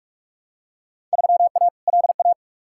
Morse code, heard as a keyed tone, about 47 wpm, 700 Hz. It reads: 3A LA
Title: Field Day Exchanges - 45wpm